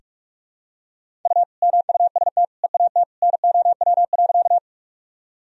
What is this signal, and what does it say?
Morse code, heard as a keyed tone, about 46 wpm, 700 Hz. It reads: U MUST EAT NOW.
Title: Sentences from Top 300 Words and Abbreviations 45wpm